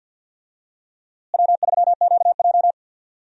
Morse code, heard as a keyed tone, about 52 wpm, 700 Hz. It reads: W3QJ
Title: USA Call Signs - 4 Characters - Repeated 3x - 50wpm